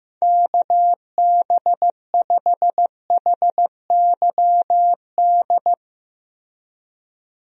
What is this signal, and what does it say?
Morse code, heard as a keyed tone, about 15 wpm, 700 Hz. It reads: KB5HYD
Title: USA Call Signs - 6 Characters - Repeated 2x - 15wpm